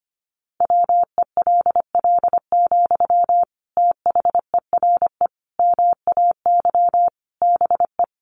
Morse code, heard as a keyed tone, about 25 wpm, 700 Hz. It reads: WELL, THERE MAY BE
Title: Sentences from Top 1300 Words 25wpm